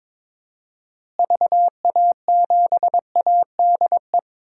Morse code, heard as a keyed tone, about 22 wpm, 700 Hz. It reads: VA7ADE
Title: DXCC-100 Call Signs - 6 Characters - Repeated 2x - 22wpm